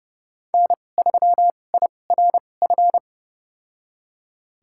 Morse code, heard as a keyed tone, about 30 wpm, spaced as for 22 wpm, 700 Hz. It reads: N3IRF
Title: USA Call Signs - 5 Characters - 2x Character Spacing - 30wpm